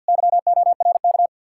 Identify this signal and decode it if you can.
Morse code, heard as a keyed tone, about 66 wpm, spaced as for 57 wpm, 700 Hz. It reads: YORK
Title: Speed-Racing Common QSO Phrases Encore - 50wpm